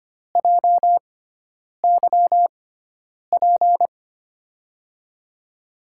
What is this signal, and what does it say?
Morse code, heard as a keyed tone, about 25 wpm, spaced as for 9 wpm, 700 Hz. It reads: JYP
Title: USA Call Signs - 6 Characters - 6x Character Spacing - 25wpm